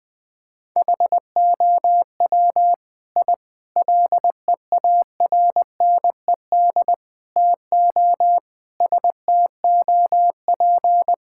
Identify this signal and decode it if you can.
Morse code, heard as a keyed tone, about 20 wpm, 700 Hz. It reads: HOW I LEARNED TO STOP
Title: Sets of 8 Words - 20wpm